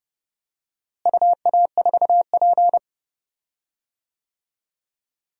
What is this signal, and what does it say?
Morse code, heard as a keyed tone, about 30 wpm, 700 Hz. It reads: UA4P
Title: DXCC-100 Call Signs - 4 Characters - Repeated 2x - 30wpm